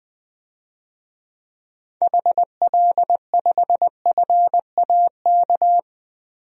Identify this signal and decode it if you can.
Morse code, heard as a keyed tone, about 20 wpm, 700 Hz. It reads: HL5FAK
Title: DXCC-100 Call Signs - 6 Characters - 20wpm